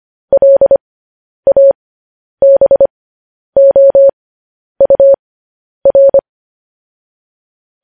Morse code, heard as a keyed tone, about 25 wpm, spaced as for 10 wpm, 550 Hz. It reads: LABOUR
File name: podcast / News headlines-1x-25x10